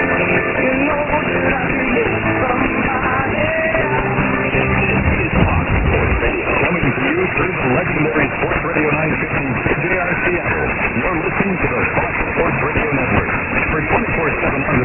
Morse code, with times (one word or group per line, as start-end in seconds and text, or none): none